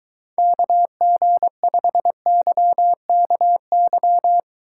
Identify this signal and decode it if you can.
Morse code, heard as a keyed tone, about 23 wpm, 700 Hz. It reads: KG5YKY